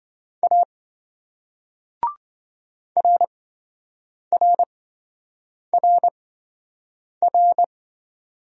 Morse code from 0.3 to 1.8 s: A

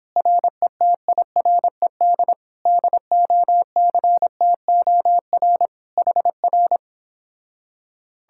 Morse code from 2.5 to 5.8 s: DOCTOR

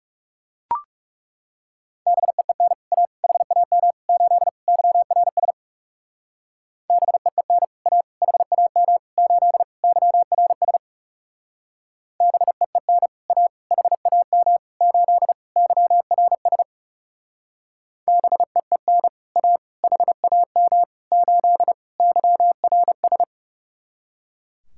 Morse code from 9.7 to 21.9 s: YRS BEEN A HAM 8 YRS BEEN A HAM 8

